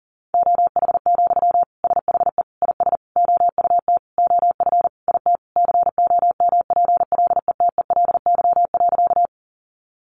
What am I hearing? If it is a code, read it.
Morse code, heard as a keyed tone, about 37 wpm, 700 Hz. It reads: OH, SHE IS OUT OF IT COMPLETELY.
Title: Sentences from Top 1100 Words 40wpm